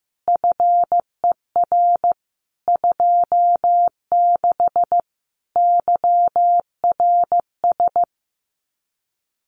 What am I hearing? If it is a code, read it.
Morse code, heard as a keyed tone, about 15 wpm, 700 Hz. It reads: FER 26 YRS
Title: Common QSO Phrases Encore 15wpm